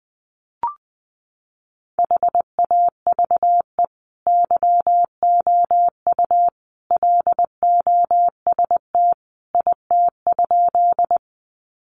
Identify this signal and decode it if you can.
Morse code, heard as a keyed tone, about 19 wpm, 700 Hz. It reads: HAVE YOU LOST IT?
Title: Sentences from Top 1400 Words 20wpm